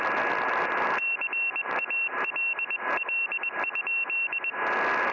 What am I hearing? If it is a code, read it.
Morse code, heard as a keyed tone, about 21 wpm, 2700 Hz, with noise. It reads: CALL?
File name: au / dj4tc 2m hrd